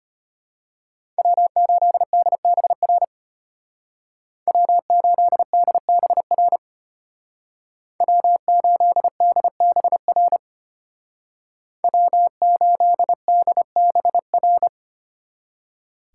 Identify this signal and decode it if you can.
Morse code, heard as a keyed tone, about 33 wpm, 700 Hz. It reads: W8DBR W8DBR W8DBR W8DBR